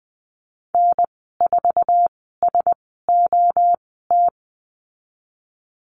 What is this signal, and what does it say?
Morse code, heard as a keyed tone, about 20 wpm, spaced as for 15 wpm, 700 Hz. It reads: N4SOT
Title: 082 Lesson 14g - Number 2 - Call Signs 20wpm